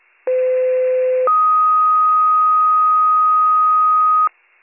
Two-tone page, heard as a one-second tone, then a three-second tone, about 510 then 1250 Hz. It is heard from 0.3 to 4.3 s.